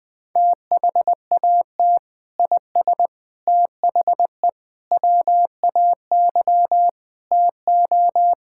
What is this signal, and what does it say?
Morse code, heard as a keyed tone, about 20 wpm, 700 Hz. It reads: THAT IS THE WAY TO
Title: Sentences with 3-Word Phrases - Spoken Beginning - 20wpm